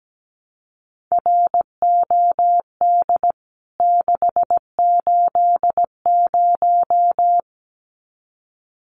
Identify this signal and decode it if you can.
Morse code, heard as a keyed tone, about 17 wpm, 700 Hz. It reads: ROD 680
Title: CWT Exchange 17wpm